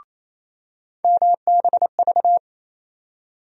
Morse code, heard as a keyed tone, about 27 wpm, 700 Hz. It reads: MBV
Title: USA Call Sign Postfixes - 3 Letters 28wpm